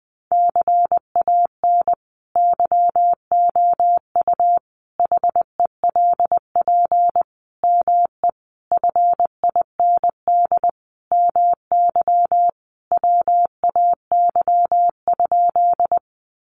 Morse code, heard as a keyed tone, about 20 wpm, 700 Hz. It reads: CAN YOU HELP ME FIND MY WAY?
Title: Sentences from Top 200 Words 20wpm